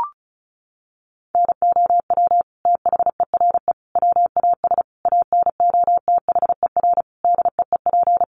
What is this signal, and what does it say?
Morse code, heard as a keyed tone, about 34 wpm, 700 Hz. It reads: NOW THERE WAS ANOTHER DEEP
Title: Sentences from Top 1300 Words 35wpm